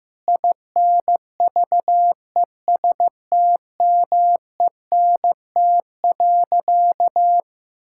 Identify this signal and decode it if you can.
Morse code, heard as a keyed tone, about 15 wpm, 700 Hz. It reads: INVESTMENT.
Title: Sentences from Top 1000 Words - Review Part 5 - 15wpm